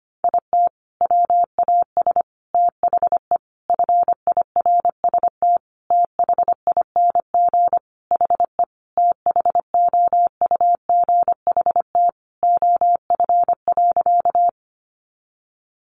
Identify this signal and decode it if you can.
Morse code, heard as a keyed tone, about 25 wpm, 700 Hz. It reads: IT WAS THE FIRST THING HE THOUGHT OF.
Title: Sentences from Top 300 Words 25wpm